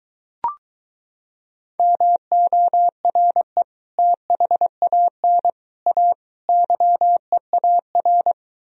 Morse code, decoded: MORE THAN A YEAR

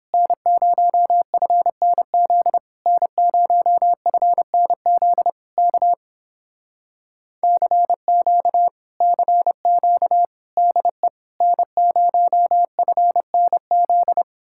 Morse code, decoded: N0FNZ N0FNZ K CQ CQ DE N0FNZ

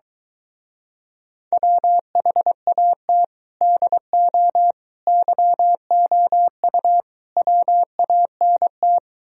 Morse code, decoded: WHAT DO YOU WANT

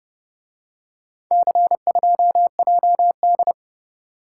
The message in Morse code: C2JD

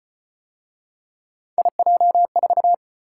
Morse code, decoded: IJ4

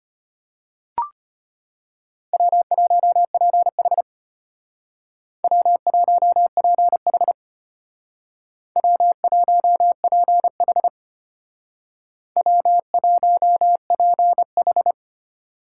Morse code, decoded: W1PH W1PH W1PH W1PH

